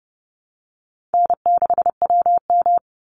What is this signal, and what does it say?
Morse code, heard as a keyed tone, about 28 wpm, 700 Hz. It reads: N6WM
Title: Call Signs Repeated 2x - 30wpm